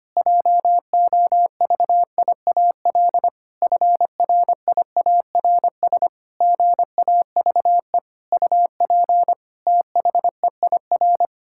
Morse code, heard as a keyed tone, about 25 wpm, 700 Hz. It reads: JOVIAL FRIARS GAVE UP THEIR
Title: Pangram Challenge v2 - 25wpm